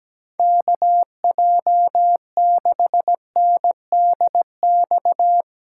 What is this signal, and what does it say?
Morse code, heard as a keyed tone, about 17 wpm, 700 Hz. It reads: KJ6NDX
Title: USA Call Signs - 8x Character Spacing - 17wpm